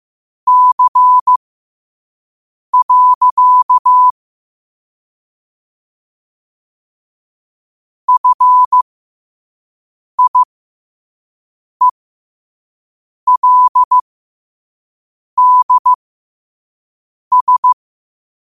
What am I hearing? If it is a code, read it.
Morse code, heard as a keyed tone, about 15 wpm, spaced as for 5 wpm, 1000 Hz. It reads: C. FIELDS
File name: qotdfiles / 5wpm